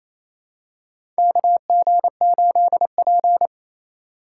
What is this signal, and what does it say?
Morse code, heard as a keyed tone, about 28 wpm, 700 Hz. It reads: KG8P
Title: USA Call Signs - 4 Characters - 2x Character Spacing - 28wpm